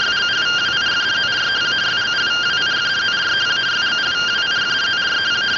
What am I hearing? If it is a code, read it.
Two-tone teleprinter signal, about 170 Hz shift.
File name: findings / MR CORONAL 55.gen-29 rtty rx